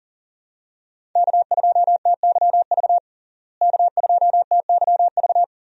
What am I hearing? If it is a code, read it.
Morse code, heard as a keyed tone, about 40 wpm, 700 Hz. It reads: K2TYV K2TYV
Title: USA Call Signs - 5 Characters - Repeated 2x - 40wpm